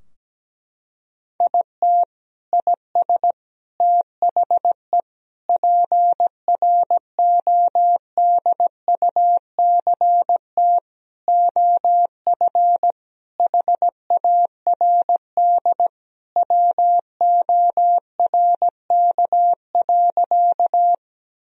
Morse code, heard as a keyed tone, about 17 wpm, 700 Hz. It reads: IT IS THE PRODUCT OF HARD WORK.
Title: Sentences from Top 1000 Words - Review Part 2 - 17wpm